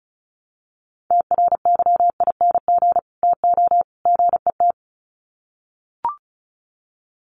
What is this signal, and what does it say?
Morse code, heard as a keyed tone, about 34 wpm, 700 Hz. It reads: TRYING TO GET
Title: Sets of 3 Words 35wpm